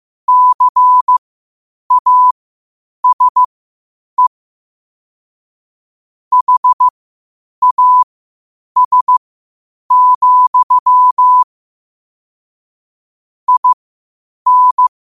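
Morse code, heard as a keyed tone, about 15 wpm, spaced as for 8 wpm, 1000 Hz. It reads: CASE HAS, IN
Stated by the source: Morse code quotes at 8 words per minute from QOTD at 8 WPM.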